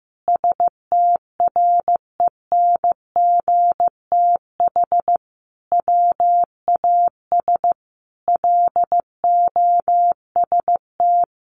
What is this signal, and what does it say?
Morse code, heard as a keyed tone, about 15 wpm, 700 Hz. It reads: STRENGTH WAS LOST
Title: Sentences from Top 1400 Words 15wpm